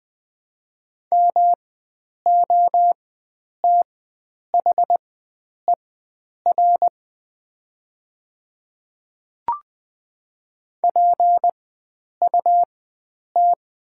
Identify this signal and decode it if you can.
Morse code, heard as a keyed tone, about 20 wpm, spaced as for 9 wpm, 700 Hz. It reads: MOTHER PUT
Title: 4x Character Spacing - Top 300 Words - 20wpm